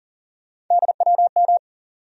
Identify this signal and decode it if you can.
Morse code, heard as a keyed tone, about 40 wpm, 700 Hz. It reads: DWM
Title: USA Call Sign Postfixes - 3 Letters 40wpm